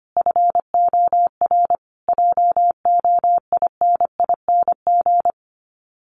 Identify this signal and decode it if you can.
Morse code, heard as a keyed tone, about 25 wpm, 700 Hz. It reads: FOR JOINING
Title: Sets of 5 Words Encore2 25wpm